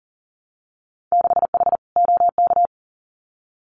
Morse code, heard as a keyed tone, about 38 wpm, 700 Hz. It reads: 6H OK
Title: Winter Field Day Exchanges Encore 40wpm